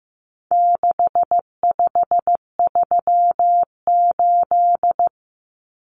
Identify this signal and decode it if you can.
Morse code, heard as a keyed tone, about 15 wpm, 700 Hz. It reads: 6538
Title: Common Cut Numbers - 4-Digit 15wpm